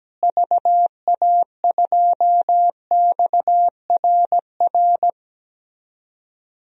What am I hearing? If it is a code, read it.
Morse code, heard as a keyed tone, about 17 wpm, 700 Hz. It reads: VA2XRR
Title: DXCC-100 Call Signs - 6 Characters - Repeated 3x - 17wpm